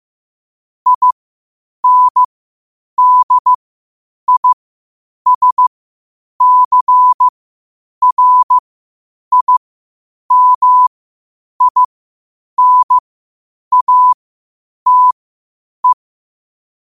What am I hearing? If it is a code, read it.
Morse code, heard as a keyed tone, about 15 wpm, spaced as for 8 wpm, 1000 Hz. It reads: INDISCRIMINATE
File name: qotdfiles / 8wpm